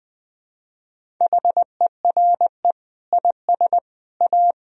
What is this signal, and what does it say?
Morse code, heard as a keyed tone, about 21 wpm, 700 Hz. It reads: HERE IS A
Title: Sentences from Top 1000 Words - Review Part 1 - 20wpm